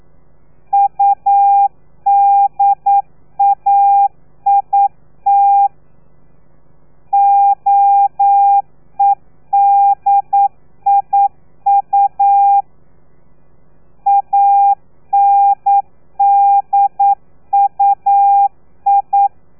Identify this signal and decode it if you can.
Morse code, heard as a keyed tone, about 9 wpm, 800 Hz. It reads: UDAIT OEDIU ANDUI
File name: telegrafia / i2tfj / Lez3 40